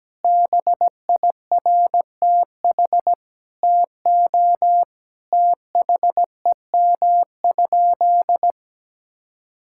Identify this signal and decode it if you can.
Morse code, heard as a keyed tone, about 17 wpm, 700 Hz. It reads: BIRTH TO THEM?
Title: Sentences from Top 1000 Words - Review Part 5 - 17wpm